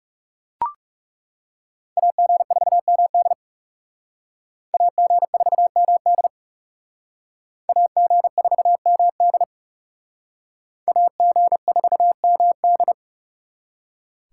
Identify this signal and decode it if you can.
Morse code, heard as a keyed tone, about 39 wpm, 700 Hz. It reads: AG4MD AG4MD AG4MD AG4MD